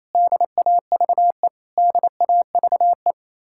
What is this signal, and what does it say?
Morse code, heard as a keyed tone, about 33 wpm, 700 Hz. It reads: DAVE DAVE
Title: Speed-Racing Common QSO Phrases Encore - 28wpm